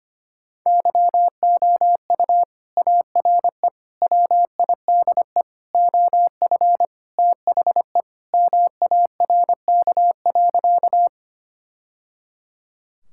Morse code, heard as a keyed tone, about 25 wpm, 700 Hz. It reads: YOU ARE WIDE OF THE MARK.
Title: Sentences from Top 1000 Words - Review Part 6 - 25wpm